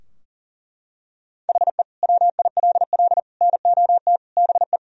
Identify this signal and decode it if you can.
Morse code, heard as a keyed tone, about 37 wpm, 700 Hz. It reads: HE WILL NOT BE
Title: Sentences from Top 1000 Words - Review Part 4 - 40wpm